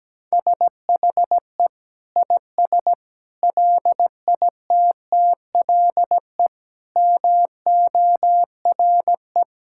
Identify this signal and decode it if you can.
Morse code, heard as a keyed tone, about 17 wpm, 700 Hz. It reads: SHE IS LITTLE MORE
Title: Sentences from Top 1600 Words 17wpm